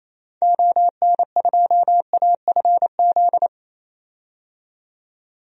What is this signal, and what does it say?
Morse code, heard as a keyed tone, about 28 wpm, 700 Hz. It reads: ON2AFZ
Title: DXCC-100 Call Signs - 6 Characters - 28wpm